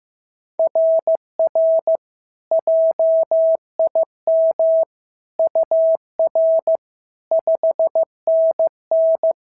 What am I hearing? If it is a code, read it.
Morse code, heard as a keyed tone, about 14 wpm, 650 Hz. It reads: RR JIM UR 5NN
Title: SOTA QSO Practice 15wpm